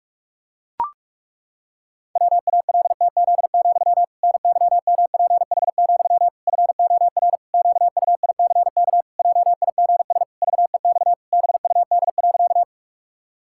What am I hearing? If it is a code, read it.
Morse code, heard as a keyed tone, about 41 wpm, 700 Hz. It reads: WALTZ, NYMPH, FOR QUICK JIGS VEX BUD.
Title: Pangram Challenge v2 - 45wpm